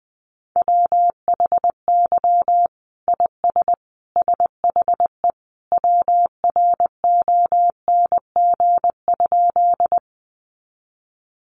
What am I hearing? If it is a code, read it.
Morse code, heard as a keyed tone, about 20 wpm, 700 Hz. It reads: WHY IS SHE WRONG?
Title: Sentences from Top 900 Words 20wpm